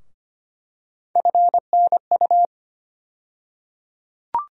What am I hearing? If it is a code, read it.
Morse code, heard as a keyed tone, about 25 wpm, 700 Hz. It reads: FNU
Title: USA Call Sign Postfixes - 3 Letters 25wpm